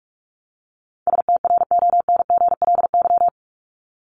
Morse code, heard as a keyed tone, about 44 wpm, 700 Hz. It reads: STRONGLY
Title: LY Suffix - Top 100 Words with Base Words 45wpm